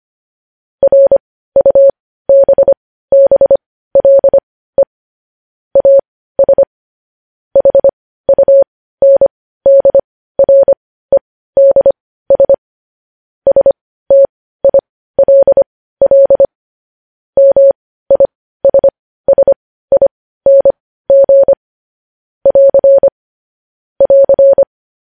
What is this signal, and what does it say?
Morse code, heard as a keyed tone, about 25 wpm, spaced as for 15 wpm, 550 Hz. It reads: RUBBLE AS HUNDREDS STILL MISSING <AR> <AR>
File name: podcast / News headlines-1x-25x15